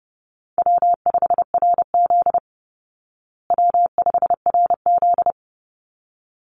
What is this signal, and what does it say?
Morse code, heard as a keyed tone, about 30 wpm, 700 Hz. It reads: W5RZ W5RZ
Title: Call Signs Repeated 3x - 30wpm